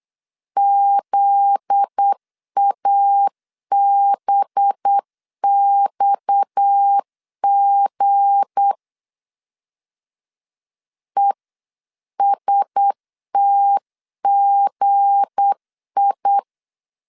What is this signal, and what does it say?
Morse code, heard as a keyed tone, about 8 wpm, 800 Hz. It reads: ZABXG ESTGI
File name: CW / DET / 1 9 1500C S C 8wpm 5653